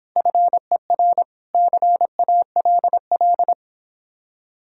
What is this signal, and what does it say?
Morse code, heard as a keyed tone, about 29 wpm, 700 Hz. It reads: FER CALL